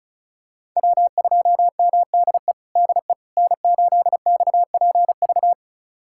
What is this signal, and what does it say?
Morse code, heard as a keyed tone, about 36 wpm, 700 Hz. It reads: W2MDE DE N8XPV
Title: USA Callsign DE Callsign - 5 Characters - 35wpm